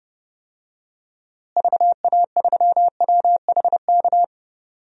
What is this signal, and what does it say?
Morse code, heard as a keyed tone, about 30 wpm, 700 Hz. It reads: VA3WHK
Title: DXCC-100 Call Signs - 6 Characters - Repeated 2x - 30wpm